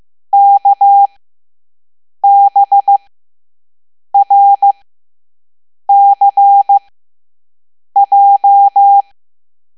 Morse code, heard as a keyed tone, about 15 wpm, spaced as for 6 wpm, 800 Hz. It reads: KBRCJ
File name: Morse / Level7-All-A